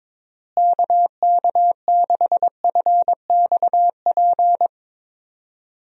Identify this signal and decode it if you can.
Morse code, heard as a keyed tone, about 22 wpm, 700 Hz. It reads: KK6FXP
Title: USA Call Signs - 6 Characters - Repeated 2x - 22wpm